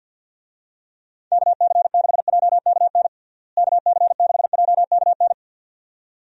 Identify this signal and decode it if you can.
Morse code, heard as a keyed tone, about 48 wpm, 700 Hz. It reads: KC6JKN KC6JKN
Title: USA Call Signs - 6 Characters - Repeated 2x - 50wpm